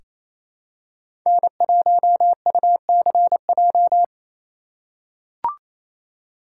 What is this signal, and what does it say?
Morse code, heard as a keyed tone, about 28 wpm, 700 Hz. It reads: N1UCJ
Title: USA Call Signs - 5 Characters - 6x Character Spacing - 28wpm